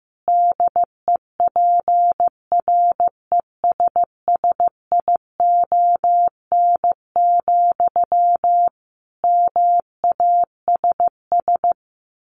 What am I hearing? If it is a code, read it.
Morse code, heard as a keyed tone, about 15 wpm, 700 Hz. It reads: DEPRESSION, MASS
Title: A Sound Of Thunder 15wpm